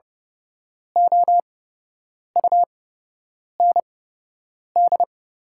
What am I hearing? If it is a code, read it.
Morse code, heard as a keyed tone, about 30 wpm, spaced as for 8 wpm, 700 Hz. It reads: OUND
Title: 8x Character Spacing - Top 300 Words - 30wpm